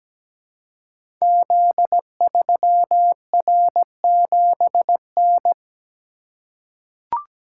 Morse code, heard as a keyed tone, about 17 wpm, 690 Hz. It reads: Z3R7N
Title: Code Groups - 5 Letter:Numbers - 17wpm